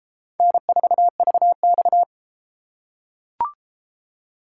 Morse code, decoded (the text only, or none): N4VX